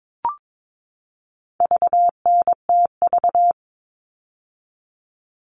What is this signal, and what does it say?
Morse code, heard as a keyed tone, about 22 wpm, 700 Hz. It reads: VNTV
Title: All Cut Numbers - 4-Digit 22wpm